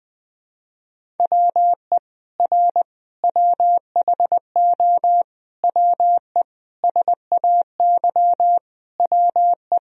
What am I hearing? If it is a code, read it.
Morse code, heard as a keyed tone, about 21 wpm, 700 Hz. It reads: WE R WHO WE SAY WE
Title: Sentences from Top 200 Words and Abbreviations 20wpm